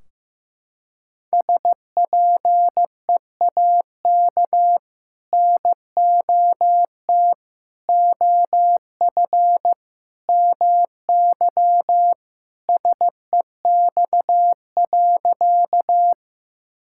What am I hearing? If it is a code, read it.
Morse code, heard as a keyed tone, about 15 wpm, 700 Hz. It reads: SPEAK NOT OF MY SEX.
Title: Sentences from Top 1600 Words 15wpm